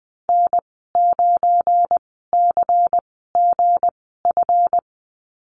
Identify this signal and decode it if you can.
Morse code, heard as a keyed tone, about 19 wpm, spaced as for 14 wpm, 700 Hz. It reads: N9CGF
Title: 213 Lesson 34g - Number 8 - Call Signs 20wpm